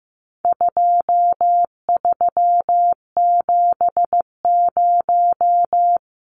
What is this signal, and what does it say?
Morse code, heard as a keyed tone, about 15 wpm, 700 Hz. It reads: 2370
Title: Numbers - 4-Digit 15wpm